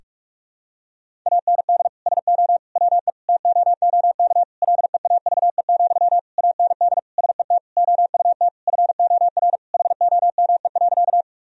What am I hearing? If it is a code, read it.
Morse code, heard as a keyed tone, about 41 wpm, 700 Hz. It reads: AND SO WE TOOK LEAVE, AND SET OUT FOR HOME.
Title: Sentences from Top 1000 Words - Review Part 1 - 45wpm